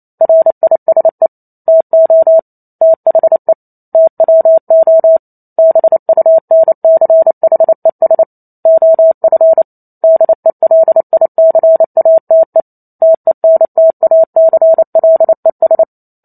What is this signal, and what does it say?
Morse code, heard as a keyed tone, about 28 wpm, 650 Hz. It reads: RISE TO THE TWO BUNCHES OF DELICATE TENTACLES